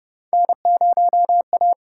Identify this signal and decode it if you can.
Morse code, heard as a keyed tone, about 30 wpm, 700 Hz. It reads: N0A